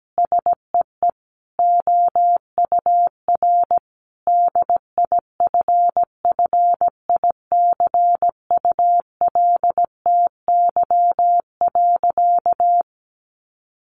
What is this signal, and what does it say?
Morse code, heard as a keyed tone, about 17 wpm, 700 Hz. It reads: SEE OUR DIFFICULTY.
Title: Sentences from Top 1500 Words 17wpm